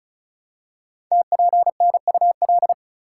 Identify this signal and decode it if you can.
Morse code, heard as a keyed tone, about 36 wpm, 690 Hz. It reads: TPNUL
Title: Code Groups - 5 Letters - 35wpm